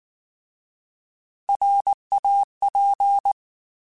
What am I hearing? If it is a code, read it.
Morse code, heard as a keyed tone, about 19 wpm, 800 Hz. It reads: RAP
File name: Alphabetical / Alphabetical mp3s / Rs @18